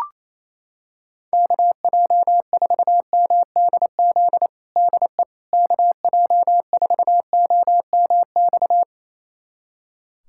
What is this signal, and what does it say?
Morse code, heard as a keyed tone, about 28 wpm, 700 Hz. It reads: KJ4MDZ DE KJ4OMX
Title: USA Callsign DE Callsign - 6 Characters - 28wpm